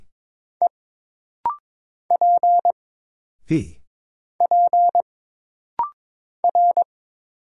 Morse code, heard as a keyed tone, about 22 wpm, 700 Hz. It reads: E P P R
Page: This rapid-fire timing is geared to push your ICR (Instant Character Recognition) skills.